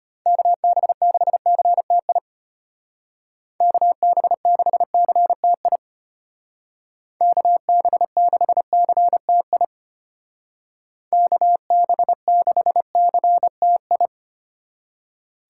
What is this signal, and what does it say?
Morse code, heard as a keyed tone, about 31 wpm, 700 Hz. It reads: KB6CTI KB6CTI KB6CTI KB6CTI
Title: Speed Racing - USA 2x3 Call Signs 25wpm